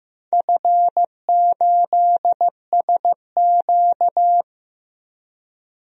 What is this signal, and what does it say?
Morse code, heard as a keyed tone, about 15 wpm, 700 Hz. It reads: F8SQ
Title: DXCC-100 Call Signs - 4 Characters - Repeated 2x - 15wpm